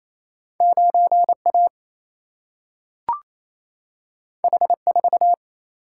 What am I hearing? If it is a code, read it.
Morse code, heard as a keyed tone, about 28 wpm, 700 Hz. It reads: 9 A H 4
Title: Single-Letter Single-Number 28wpm